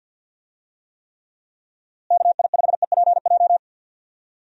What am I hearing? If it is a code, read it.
Morse code, heard as a keyed tone, about 51 wpm, 700 Hz. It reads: KI5EPJ
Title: USA Call Signs - 6 Characters - Repeated 2x - 50wpm